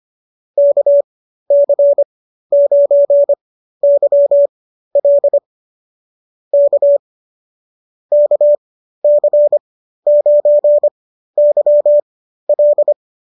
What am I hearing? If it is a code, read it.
Morse code, heard as a keyed tone, about 25 wpm, spaced as for 13 wpm, 590 Hz. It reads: KC9YL K KC9YL